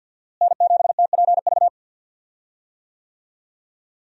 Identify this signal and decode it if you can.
Morse code, heard as a keyed tone, about 51 wpm, 700 Hz. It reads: N7TPV
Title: USA Call Signs - 5 Characters - Repeated 2x - 50wpm